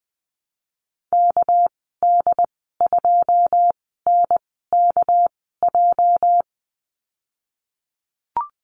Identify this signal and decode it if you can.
Morse code, heard as a keyed tone, about 20 wpm, spaced as for 15 wpm, 700 Hz. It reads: KD2NKJ
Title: 238 Lesson 38g - Letter q - Call Signs 20wpm